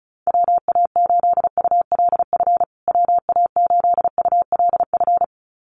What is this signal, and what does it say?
Morse code, heard as a keyed tone, about 34 wpm, 700 Hz. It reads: WA8ULF WA8ULF
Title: USA Call Signs Repeated 2x 35wpm